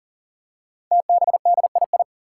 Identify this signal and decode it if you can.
Morse code, heard as a keyed tone, about 40 wpm, 700 Hz. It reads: TBDII
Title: Five Challenging Letter-Number Combinations 40wpm